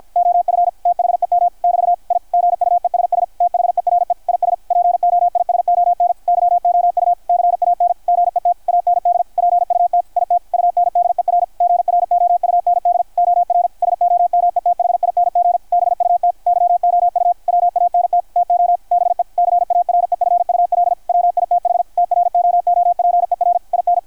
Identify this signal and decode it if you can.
Morse code, heard as a keyed tone, about 53 wpm, 690 Hz, with radio noise. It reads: OF THEM <BT> I GUESS THERE IS POISON YOU CAN GET AND PUT IT UNDER GROUND OR SOMETHING BUT YOU WANT TO BE CAREFUL WITH TROOPER IN
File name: archived / CRSnet